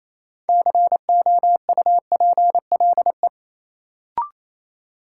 Morse code, heard as a keyed tone, about 28 wpm, 700 Hz. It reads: COUPLE
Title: Sets of 2 Words - New Word Drill - 28wpm